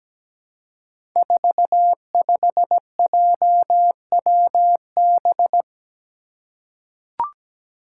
Morse code, heard as a keyed tone, about 17 wpm, 700 Hz. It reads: 45JWB
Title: Five Challenging Letter-Number Combinations 17wpm